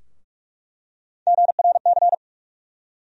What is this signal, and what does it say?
Morse code, heard as a keyed tone, about 45 wpm, 700 Hz. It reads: GRC
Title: USA Call Sign Postfixes - 3 Letters 45wpm